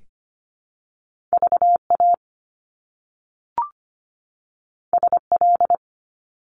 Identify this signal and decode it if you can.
Morse code, heard as a keyed tone, about 25 wpm, 700 Hz. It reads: VA SL
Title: Top Two Letter Combinations in Words 25wpm